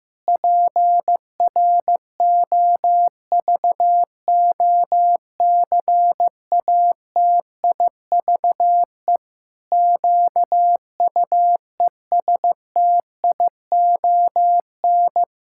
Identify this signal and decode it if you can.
Morse code, heard as a keyed tone, about 15 wpm, 700 Hz. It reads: PROVOCATIVE QUESTION
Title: Pangram Challenge v2 - 15wpm